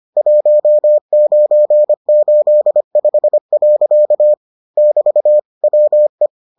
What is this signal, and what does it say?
Morse code, heard as a keyed tone, about 25 wpm, 600 Hz. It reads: 1985. <BT> WE